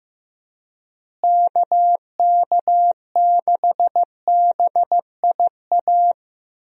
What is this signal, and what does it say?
Morse code, heard as a keyed tone, about 16 wpm, 700 Hz. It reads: KK6BIA
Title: USA Call Signs - 6 Characters - Repeated 3x - 15wpm